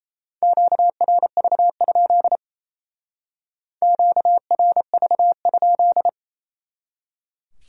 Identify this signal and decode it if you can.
Morse code, heard as a keyed tone, about 37 wpm, 700 Hz. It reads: QRV? QRV?